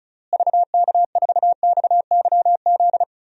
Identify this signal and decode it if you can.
Morse code, heard as a keyed tone, about 34 wpm, 700 Hz. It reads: VK4XYZ
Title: DXCC-100 Call Signs - 6 Characters - Repeated 3x - 35wpm